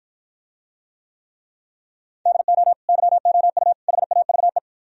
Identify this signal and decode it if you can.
Morse code, heard as a keyed tone, about 54 wpm, 700 Hz. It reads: DO YOU HAVE